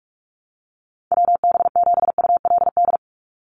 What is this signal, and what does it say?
Morse code, heard as a keyed tone, about 44 wpm, 700 Hz. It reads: WB7ULD
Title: NA SOTA Chaser Callsigns 45wpm